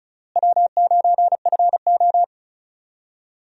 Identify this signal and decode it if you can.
Morse code, heard as a keyed tone, about 36 wpm, 700 Hz. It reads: W9FO
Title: USA Call Signs - 4 Characters - Repeated 3x - 35wpm